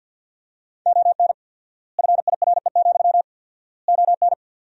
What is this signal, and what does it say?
Morse code, heard as a keyed tone, about 52 wpm, 700 Hz. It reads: ON FIRE, ON
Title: 4x Word-Spacing - Sentences from Top 400 Words - 50wpm